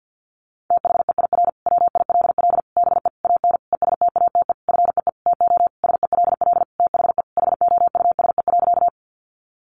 Morse code, heard as a keyed tone, about 50 wpm, 700 Hz. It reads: THEIR WILL BE AN ESTATE FEE TO SELL THE HOUSE.
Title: Sentences from Top 1500 Words 50wpm